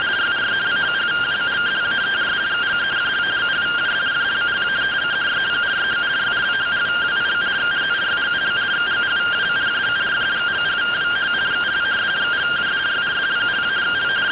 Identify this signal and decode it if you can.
Two-tone teleprinter signal, about 170 Hz shift.